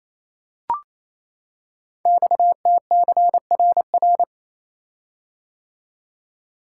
Morse code, heard as a keyed tone, about 27 wpm, 700 Hz. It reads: XTCRR
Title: Code Groups - 5 Letters - 28wpm